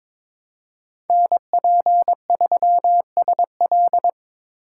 Morse code, heard as a keyed tone, about 22 wpm, 700 Hz. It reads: NP3SL
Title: USA Call Signs - 5 Characters - 4x Character Spacing - 22wpm